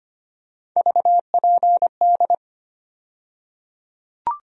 Morse code, heard as a keyed tone, about 25 wpm, 700 Hz. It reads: VPD
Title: USA Call Sign Postfixes - 3 Letters 25wpm